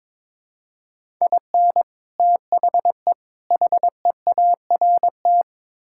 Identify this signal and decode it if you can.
Morse code, heard as a keyed tone, about 22 wpm, 700 Hz. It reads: IN THE HEART
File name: practice / 22wpm / Build A Phrase - 5-Word Phrases - 22wpm